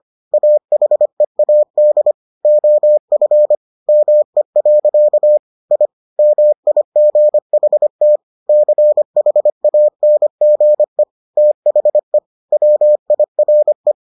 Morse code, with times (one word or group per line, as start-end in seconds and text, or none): 0.3-2.1 AHEAD
2.5-3.6 OF
3.9-5.4 ME.
5.7-5.9 I
6.2-8.2 MIGHT
8.5-11.0 CHANGE
11.4-12.2 THE
12.5-13.9 WIRE